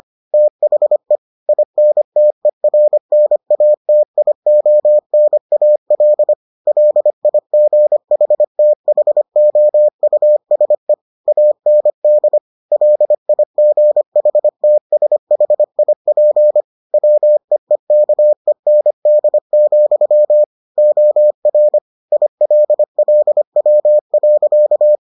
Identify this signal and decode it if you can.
Morse code, heard as a keyed tone, about 25 wpm, 600 Hz. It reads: THE INTERNATIONAL LIGHTHOUSE AND LIGHTSHIP WEEKEND, OR ILLW.